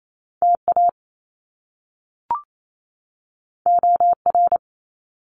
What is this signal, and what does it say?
Morse code, heard as a keyed tone, about 28 wpm, 700 Hz. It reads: TA OR
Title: Top Two Letter Combinations in Words 28wpm